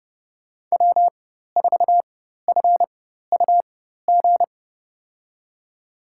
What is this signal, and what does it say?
Morse code, heard as a keyed tone, about 30 wpm, spaced as for 14 wpm, 700 Hz. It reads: W4FUG
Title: USA Call Signs - 5 Characters - 4x Character Spacing - 30wpm